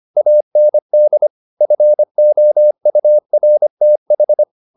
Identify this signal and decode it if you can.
Morse code, heard as a keyed tone, about 25 wpm, 600 Hz. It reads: AND FOURTH
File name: the old man and the sea part 1 25wpm